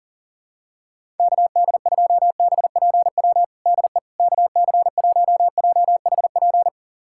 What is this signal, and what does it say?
Morse code, heard as a keyed tone, about 37 wpm, 700 Hz. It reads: KD2BPW DE KC1JHP
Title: USA Callsign DE Callsign 40wpm